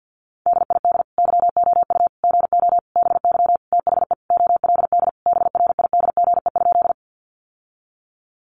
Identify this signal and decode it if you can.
Morse code, heard as a keyed tone, about 50 wpm, 700 Hz. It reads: DID YOU GO BY THE OLD BRIDGE?
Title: Sentences from Top 1200 Words 50wpm